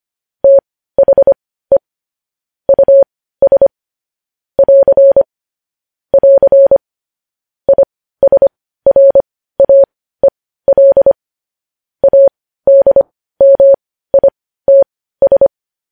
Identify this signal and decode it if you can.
Morse code, heard as a keyed tone, about 25 wpm, spaced as for 15 wpm, 550 Hz. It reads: THE US <AR> <AR> ISRAEL ADMITS
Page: news-headlines-in-morse-code-at-15-wpm